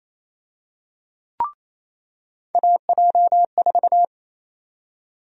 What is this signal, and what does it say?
Morse code, heard as a keyed tone, about 28 wpm, 700 Hz. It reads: AJ4
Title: USA Call Sign Prefixes - 28wpm